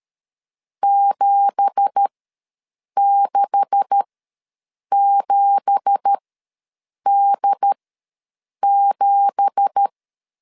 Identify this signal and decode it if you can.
Morse code, heard as a keyed tone, about 13 wpm, spaced as for 7 wpm, 800 Hz. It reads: 767D7
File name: LEZIONI / 9 2867 ws 12wpm 3431